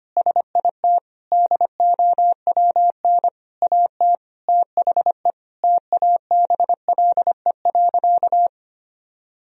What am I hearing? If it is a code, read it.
Morse code, heard as a keyed tone, about 25 wpm, 700 Hz. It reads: SIT DOWN AT THE TABLE.
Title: Sentences from Top 1000 Words - Review Part 2 - 25wpm